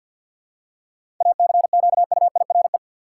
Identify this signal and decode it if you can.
Morse code, heard as a keyed tone, about 52 wpm, 700 Hz. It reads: ACQUIRE
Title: Seven Letter Words - 50wpm